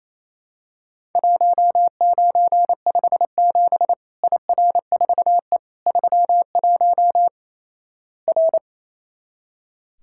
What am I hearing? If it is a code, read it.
Morse code, heard as a keyed tone, about 28 wpm, 700 Hz. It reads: 1957 IR4E 31 R
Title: Worked All Europe DX Contest - QTC Exchange - 28wpm